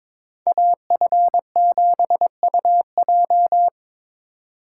Morse code, heard as a keyed tone, about 26 wpm, 700 Hz. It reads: AF7UJ
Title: Speed Racing - USA 2x2 Call Signs 22wpm